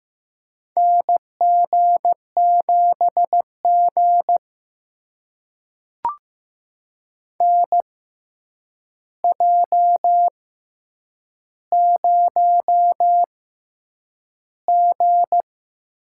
Morse code, heard as a keyed tone, about 15 wpm, spaced as for 8 wpm, 700 Hz. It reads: NG7G NJ0G
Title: USA Call Signs - 4 Characters - 6x Character Spacing - 15wpm